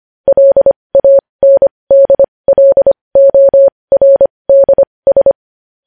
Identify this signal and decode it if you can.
Morse code, heard as a keyed tone, about 25 wpm, spaced as for 20 wpm, 550 Hz. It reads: LANDLORDS
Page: news headlines in morse code at 20 wpm